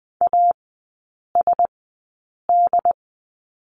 Morse code, heard as a keyed tone, about 20 wpm, spaced as for 8 wpm, 700 Hz. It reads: ASD
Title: 067 Lesson 12e - Letter d - Three Characters 20wpm